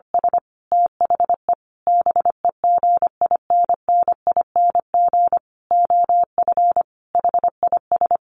Morse code, heard as a keyed tone, about 25 wpm, 700 Hz. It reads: S THE BEGINNING OF HIS
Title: Sentences from Top 900 Words 25wpm